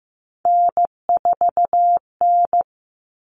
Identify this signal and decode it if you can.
Morse code, heard as a keyed tone, about 15 wpm, 700 Hz. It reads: N4N
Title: All Cut Numbers - 3-Digit 15wpm